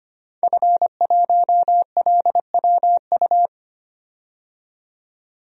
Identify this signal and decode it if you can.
Morse code, heard as a keyed tone, about 25 wpm, 700 Hz. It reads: F1LWU
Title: DXCC-100 Call Signs - 5 Characters - 25wpm